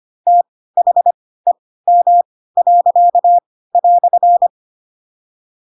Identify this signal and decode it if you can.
Morse code, heard as a keyed tone, about 25 wpm, spaced as for 16 wpm, 700 Hz. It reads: THEM."
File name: Pinocchio / Pinocchio ch7 10 25 ik1hge